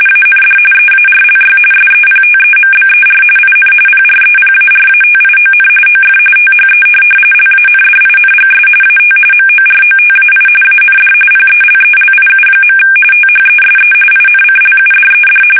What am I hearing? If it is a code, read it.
RTTY, about 45.45 baud. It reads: AND EARLY SEPTEMBER IN PREPARATION OF THE LANDINGS AT MOROTAI AND PELELIU. THESE INCLUDED
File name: fsk850 / USS-Massachusetts